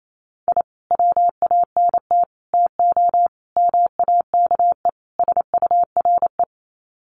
Morse code, heard as a keyed tone, about 28 wpm, 700 Hz. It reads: I WANT TO MAKE SURE
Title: Sets of 5 Words Encore2 28wpm